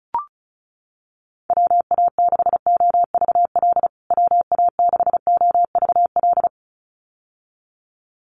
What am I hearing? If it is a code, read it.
Morse code, heard as a keyed tone, about 34 wpm, 700 Hz. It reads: WA6OVL WA6OVL
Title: USA Call Signs Repeated 2x 35wpm